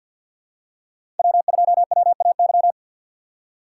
Morse code, heard as a keyed tone, about 50 wpm, 700 Hz. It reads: W2WAY
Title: USA Call Signs - 5 Characters - 3x Character Spacing - 50wpm